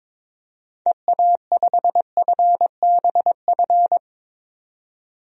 Morse code, heard as a keyed tone, about 22 wpm, 700 Hz. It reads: EA5FBF
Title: DXCC-100 Call Signs - 6 Characters - 22wpm